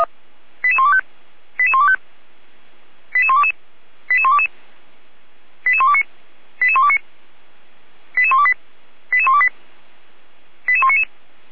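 Five-tone selective call (ZVEI), heard as quick runs of five tones, tones of about 80 ms. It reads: 88125 88125 88122 88122 88129 88129 88128 88128 88199
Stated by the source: ZVEI five-tone selective call 88235 received on SDR.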